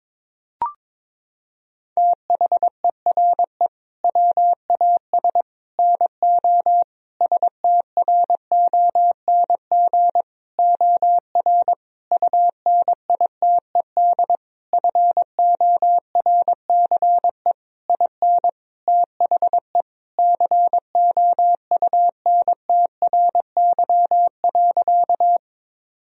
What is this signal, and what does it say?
Morse code, heard as a keyed tone, about 22 wpm, 700 Hz. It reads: THERE WAS NO STRONG OR UNITED FORCE IN THE COUNTRY.
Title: Sentences from Top 1000 Words - Review Part 4 - 22wpm